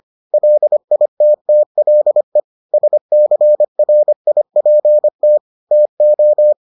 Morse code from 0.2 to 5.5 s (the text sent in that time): LITTLE SCRIPT